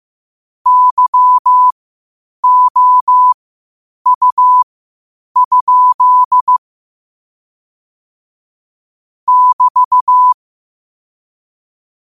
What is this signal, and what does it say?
Morse code, heard as a keyed tone, about 15 wpm, spaced as for 8 wpm, 1000 Hz. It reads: YOU? <BT>
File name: qotdfiles / 8wpm